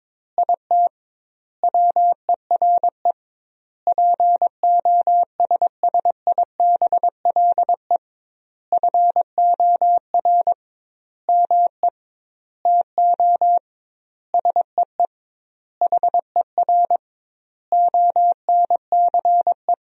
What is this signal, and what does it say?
Morse code, heard as a keyed tone, about 22 wpm, 700 Hz. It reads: IT WERE POSSIBLE FOR ME TO SEE HER ONCE
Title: 2x Word-Spacing - Sentences from Top 500 Words - 22wpm